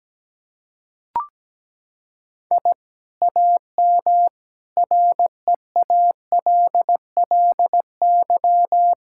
Morse code, heard as a keyed tone, about 17 wpm, 700 Hz. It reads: I AM REALLY
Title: Sentences from Top 1000 Words - Review Part 6 - 17wpm